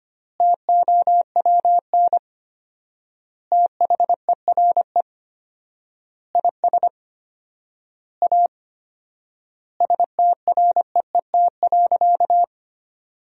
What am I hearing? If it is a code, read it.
Morse code, heard as a keyed tone, about 25 wpm, 700 Hz. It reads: TOWN THERE IS A STREET.
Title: 4x Word-Spacing - Sentences from Top 500 Words - 25wpm